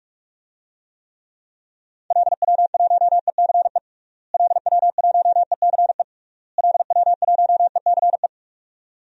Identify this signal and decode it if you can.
Morse code, heard as a keyed tone, about 46 wpm, 700 Hz. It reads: LW1ECE LW1ECE LW1ECE
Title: DXCC-100 Call Signs - 6 Characters - Repeated 3x - 45wpm